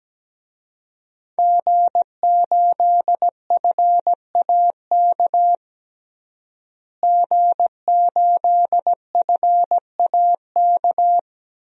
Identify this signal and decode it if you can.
Morse code, heard as a keyed tone, about 17 wpm, 700 Hz. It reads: G8FAK G8FAK
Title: DXCC-100 Call Signs - 5 Characters - Repeated 2x - 17wpm